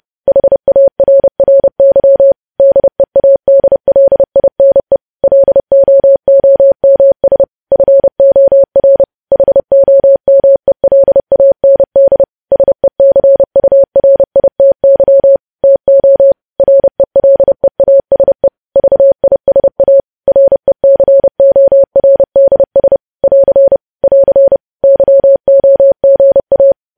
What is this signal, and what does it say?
Morse code, heard as a keyed tone, about 30 wpm, 550 Hz. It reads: HARRY DEADLINE LOOMS FOR HOMELAND SECURITY TO RELEASE VISA RECORDS <AR> <AR> YOGA
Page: news headlines in morse code at 30 wpm